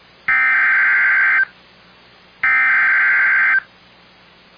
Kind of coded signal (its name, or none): Emergency Alert System (SAME) tones